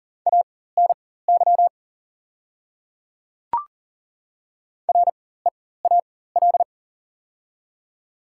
Morse code, heard as a keyed tone, about 40 wpm, spaced as for 19 wpm, 700 Hz. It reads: ANY REAL
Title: 4x Character Spacing - Top 500 Words - 40wpm